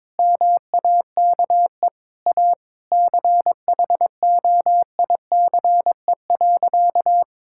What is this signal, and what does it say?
Morse code, heard as a keyed tone, about 22 wpm, 700 Hz. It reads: MAKE A CHOICE.
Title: Sentences from Top 1000 Words - Review Part 6 - 22wpm